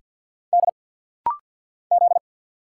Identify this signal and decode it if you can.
Morse code, heard as a keyed tone, about 50 wpm, 700 Hz. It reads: D Z
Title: Single Letters - Rapid-Fire 50wpm